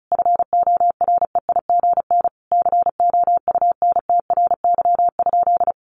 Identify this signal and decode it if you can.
Morse code, heard as a keyed tone, about 34 wpm, 700 Hz. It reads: FOREIGN COUNTRY?
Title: Sentences from Top 1400 Words 35wpm